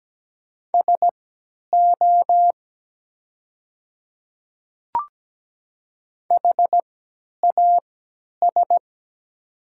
Morse code, heard as a keyed tone, about 17 wpm, spaced as for 10 wpm, 700 Hz. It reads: SO HAS
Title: 3x Character Spacing - Top 100 Words - 17wpm